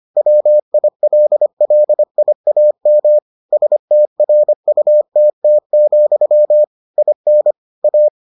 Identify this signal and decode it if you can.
Morse code, heard as a keyed tone, about 25 wpm, 600 Hz. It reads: WILLIAM STRUTT, IN A